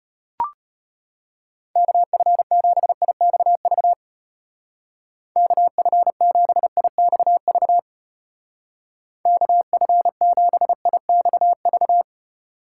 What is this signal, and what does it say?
Morse code, heard as a keyed tone, about 33 wpm, 700 Hz. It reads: KF7IXV KF7IXV KF7IXV